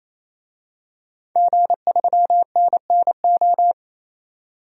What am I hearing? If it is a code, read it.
Morse code, heard as a keyed tone, about 28 wpm, 700 Hz. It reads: G3NNO
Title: DXCC-100 Call Signs - 5 Characters - 28wpm